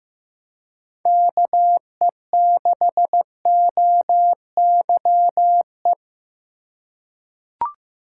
Morse code, heard as a keyed tone, about 15 wpm, 700 Hz. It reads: KE6OYE
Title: USA Call Signs - 6 Characters - 4x Character Spacing - 15wpm